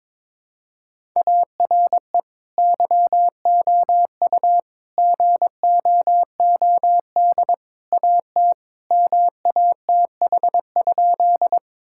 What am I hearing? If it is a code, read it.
Morse code, heard as a keyed tone, about 22 wpm, 700 Hz. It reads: ARE YOU GOOD AT MATH?
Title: Sentences from Top 1000 Words - Review Part 6 - 22wpm